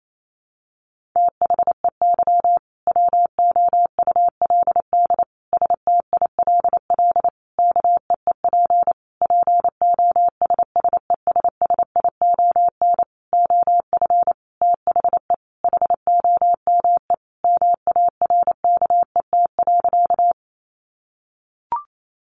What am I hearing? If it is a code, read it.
Morse code, heard as a keyed tone, about 28 wpm, 700 Hz. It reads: THEY WOULD STILL KEEP POSSESSION OF THE HOME MARKET.
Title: Sentences from Top 1400 Words 28wpm